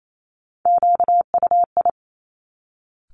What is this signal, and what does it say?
Morse code, heard as a keyed tone, about 28 wpm, 700 Hz. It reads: QUI
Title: Top Three Letter Combinations in Words 28wpm